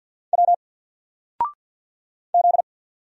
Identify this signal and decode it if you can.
Morse code, heard as a keyed tone, about 48 wpm, 700 Hz. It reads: W Z
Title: Single Letter-Number Alt 50wpm